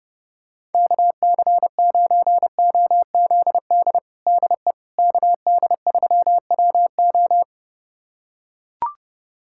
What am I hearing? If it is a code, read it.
Morse code, heard as a keyed tone, about 32 wpm, 700 Hz. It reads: KC9OZD DE KD3WO
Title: USA Callsign DE Callsign 30wpm